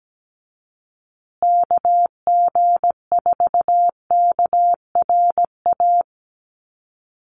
Morse code, heard as a keyed tone, about 17 wpm, 700 Hz. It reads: KG4KRA
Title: USA Call Signs Repeated 2x 17wpm